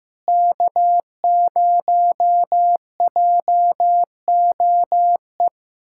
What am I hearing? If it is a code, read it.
Morse code, heard as a keyed tone, about 15 wpm, 700 Hz. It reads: K0JOE
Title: USA Callsign DE Callsign 15wpm